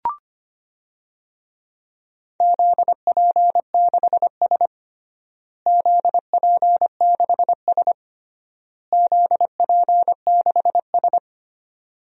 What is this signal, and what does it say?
Morse code, heard as a keyed tone, about 25 wpm, 700 Hz. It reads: ZP6S ZP6S ZP6S
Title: DXCC-100 Call Signs - 4 Characters - Repeated 3x - 25wpm